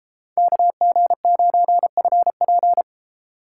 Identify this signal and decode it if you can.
Morse code, heard as a keyed tone, about 32 wpm, 700 Hz. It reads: KG9FP